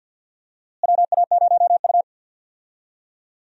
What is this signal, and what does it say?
Morse code, heard as a keyed tone, about 51 wpm, 700 Hz. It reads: WA0U
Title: USA Call Signs - 4 Characters - Repeated 3x - 50wpm